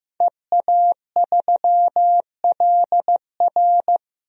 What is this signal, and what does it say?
Morse code, heard as a keyed tone, about 15 wpm, 700 Hz. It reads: EA3LR
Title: DXCC-100 Call Signs - 5 Characters - 15wpm